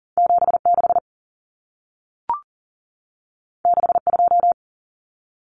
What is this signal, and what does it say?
Morse code, heard as a keyed tone, about 40 wpm, 700 Hz. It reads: 76 62
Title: Common Cut Numbers - 2-Digit 40wpm